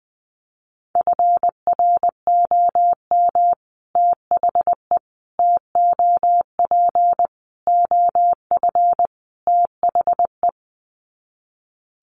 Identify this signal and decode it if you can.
Morse code, heard as a keyed tone, about 20 wpm, 700 Hz. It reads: FROM THE TOP OF THE
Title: Sets of 5 Words Encore 20wpm